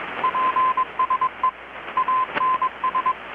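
Morse code, heard as a keyed tone, about 21 wpm, 1030 Hz, with radio noise. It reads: PSE ANS
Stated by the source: Morse from KPH's Night of Nights XX maritime radiotelegraphy event, received in Chiapas, Mexico about 3,800 km away on a Kenwood R-600 with a grounded 10 m long-wire antenna.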